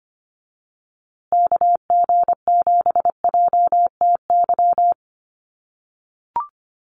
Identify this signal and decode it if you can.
Morse code, heard as a keyed tone, about 25 wpm, 700 Hz. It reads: KG7JTY
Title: USA Call Signs Repeated 2x 25wpm